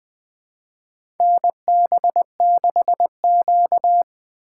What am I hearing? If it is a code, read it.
Morse code, heard as a keyed tone, about 20 wpm, 700 Hz. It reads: NB6Q
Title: USA Call Signs - 4 Characters - 4x Character Spacing - 20wpm